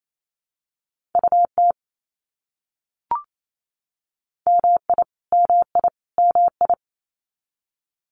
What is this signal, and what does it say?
Morse code, heard as a keyed tone, about 28 wpm, 700 Hz. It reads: UT MI MI MI
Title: US States Repeated 3x 28wpm